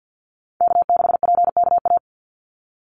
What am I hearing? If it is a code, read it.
Morse code, heard as a keyed tone, about 50 wpm, 700 Hz. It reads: K6PKA
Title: USA Call Signs 50wpm